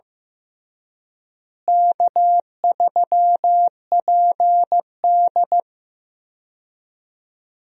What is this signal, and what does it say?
Morse code, heard as a keyed tone, about 15 wpm, 700 Hz. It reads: K3PD
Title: USA Call Signs - 4 Characters - Repeated 3x - 15wpm